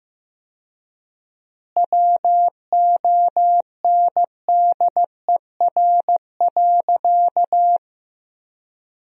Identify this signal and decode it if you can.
Morse code, heard as a keyed tone, about 15 wpm, 700 Hz. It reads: WONDER.
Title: 4x Word-Spacing - Sentences from Top 500 Words - 15wpm